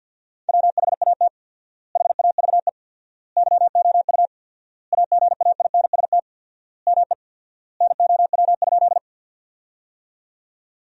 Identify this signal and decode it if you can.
Morse code, heard as a keyed tone, about 50 wpm, 700 Hz. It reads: WHAT HAVE YOU AGAINST ME NOW?
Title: 4x Word-Spacing - Sentences from Top 400 Words - 50wpm